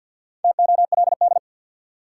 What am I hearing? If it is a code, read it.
Morse code, heard as a keyed tone, about 51 wpm, 700 Hz. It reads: TOLD
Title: Sentences from Top 400 Words - New Word Drill - 50wpm